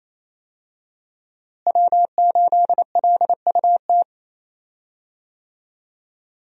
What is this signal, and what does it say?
Morse code, heard as a keyed tone, about 28 wpm, 700 Hz. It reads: W8LUT
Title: USA Call Signs - 5 Characters - Repeated 3x - 28wpm